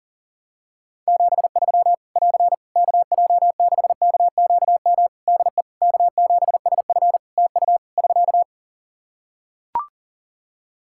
Morse code, heard as a keyed tone, about 40 wpm, 700 Hz. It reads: 73 <AR> KJ6KQM DE K7SF TU <SK>
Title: Practice QSO - Exchange 4 - Ending - 40wpm